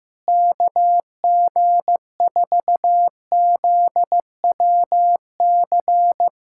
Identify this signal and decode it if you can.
Morse code, heard as a keyed tone, about 15 wpm, 700 Hz. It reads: KG4ZWC
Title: USA Callsign DE Callsign - 6 Characters - 15wpm